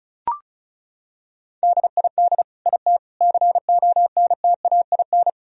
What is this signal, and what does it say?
Morse code, heard as a keyed tone, about 36 wpm, 700 Hz. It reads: DID IT CONTAIN
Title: Sentences from Top 1000 Words - Review Part 2 - 35wpm